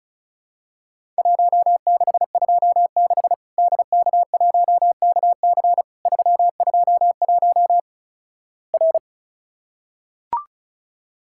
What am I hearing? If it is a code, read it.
Morse code, heard as a keyed tone, about 36 wpm, 700 Hz. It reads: 1626 DK1KC 321 R
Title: Worked All Europe DX Contest - QTC Exchange - 35wpm